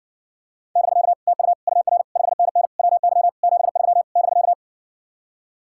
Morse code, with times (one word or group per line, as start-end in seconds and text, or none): 0.8-1.1 <BK>
1.3-1.5 TU
1.7-2.0 UR
2.2-2.7 5NN
2.8-3.3 WY
3.4-4.0 73
4.2-4.5 <BK>